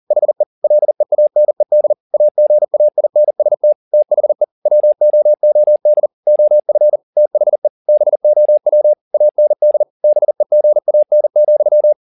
Morse code, heard as a keyed tone, about 40 wpm, 600 Hz. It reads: HE LEANED AGAINST THE WOOD OF THE BOW AND BEGAN,